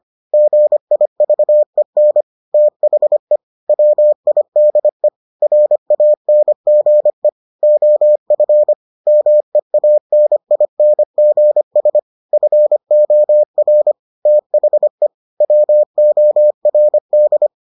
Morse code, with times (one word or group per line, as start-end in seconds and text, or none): 0.3-2.2 GIVEN
2.5-3.4 THE
3.7-5.1 WIDE
5.4-7.3 RANGE
7.6-8.7 OF
9.1-12.0 MEANINGS
12.3-13.9 FOR
14.3-15.1 THE
15.4-17.5 WORD